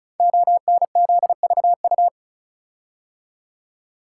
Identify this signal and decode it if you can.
Morse code, heard as a keyed tone, about 36 wpm, 690 Hz. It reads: ONZVU
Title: Code Groups - 5 Letter:Numbers - 35wpm